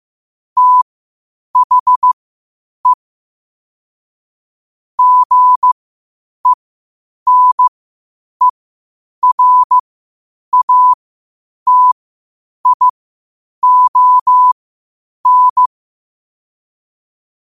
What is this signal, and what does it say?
Morse code, heard as a keyed tone, about 15 wpm, spaced as for 8 wpm, 1000 Hz. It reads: THE GENERATION
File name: qotdfiles / 8wpm